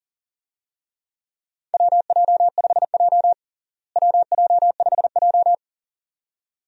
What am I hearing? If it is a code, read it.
Morse code, heard as a keyed tone, about 40 wpm, 700 Hz. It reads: WJ5J WJ5J
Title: USA Call Signs - 4 Characters - Repeated 2x - 40wpm